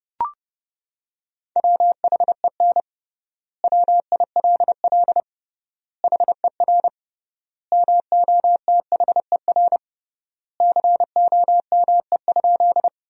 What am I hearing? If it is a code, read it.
Morse code, heard as a keyed tone, about 30 wpm, 700 Hz. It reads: WHEN WILL HER MOTHER COME?
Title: 3x Word-Spacing - Sentences from Top 200 Words - 30wpm